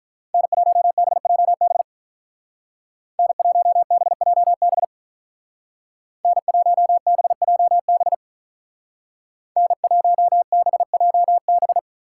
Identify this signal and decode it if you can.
Morse code, heard as a keyed tone, about 45 wpm, 700 Hz. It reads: N1BJB N1BJB N1BJB N1BJB